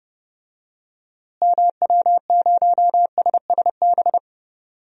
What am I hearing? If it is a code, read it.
Morse code, heard as a keyed tone, about 30 wpm, 700 Hz. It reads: MW0SSB
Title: DXCC-100 Call Signs - 6 Characters - Repeated 3x - 30wpm